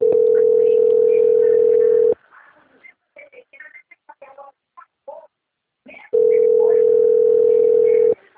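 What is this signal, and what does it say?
Ringback tone (North America), 440 + 480 Hz.
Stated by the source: Telephone call placed from the Valinhos - SP system.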